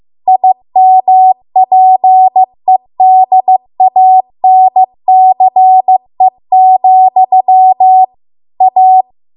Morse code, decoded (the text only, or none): IMPEDANCE, A